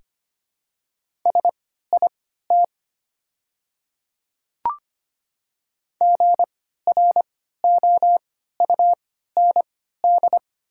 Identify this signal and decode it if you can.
Morse code, heard as a keyed tone, about 25 wpm, spaced as for 14 wpm, 690 Hz. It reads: SIT GROUND
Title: 3x Character Spacing - Top 500 Words - 25wpm